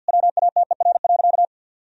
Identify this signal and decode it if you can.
Morse code, heard as a keyed tone, about 50 wpm, 700 Hz. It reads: WATER.
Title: Sentences from Top 1000 Words - Review Part 1 - 50wpm